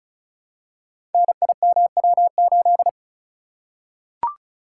Morse code, heard as a keyed tone, about 36 wpm, 690 Hz. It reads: NIMW8
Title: Code Groups - 5 Letter:Numbers - 35wpm